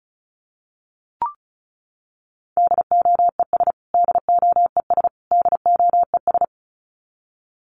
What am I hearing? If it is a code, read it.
Morse code, heard as a keyed tone, about 34 wpm, 700 Hz. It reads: DOES DOES DOES
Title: Top 200 Words Repeated 35wpm